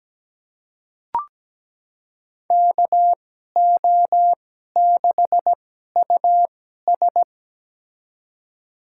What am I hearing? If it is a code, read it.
Morse code, heard as a keyed tone, about 17 wpm, spaced as for 12 wpm, 700 Hz. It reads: KO6US
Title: USA Call Signs - 5 Characters - 2x Character Spacing - 17wpm